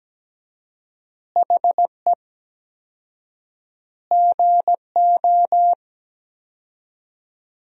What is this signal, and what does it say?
Morse code, heard as a keyed tone, about 17 wpm, 700 Hz. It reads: HE GO
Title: 4x Word-Spacing - Sentences from Top 500 Words - 17wpm